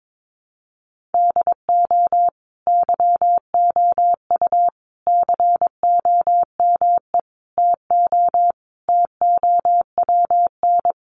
Morse code, heard as a keyed tone, about 22 wpm, 700 Hz. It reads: DO YOU COME TO TOWN
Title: Sentences from Top 1400 Words 22wpm